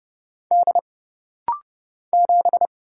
Morse code, heard as a keyed tone, about 30 wpm, 700 Hz. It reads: D 7
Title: Single Letter-Number Alt - Rapid-Fire 30wpm